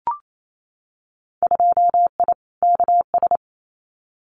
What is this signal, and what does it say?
Morse code, heard as a keyed tone, about 28 wpm, 700 Hz. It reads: 2I KS
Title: Winter Field Day Exchanges Encore 28wpm